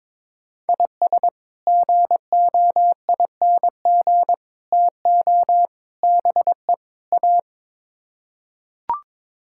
Morse code, heard as a keyed tone, about 22 wpm, 700 Hz. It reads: IS GOING TO BE A
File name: practice / 22wpm / Build A Phrase - 5-Word Phrases - 22wpm